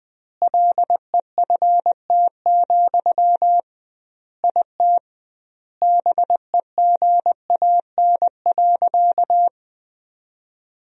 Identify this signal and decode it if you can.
Morse code, heard as a keyed tone, about 21 wpm, 700 Hz. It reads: LEFT, IT BEGAN.
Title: 2x Word-Spacing - Sentences from Top 400 Words - 20wpm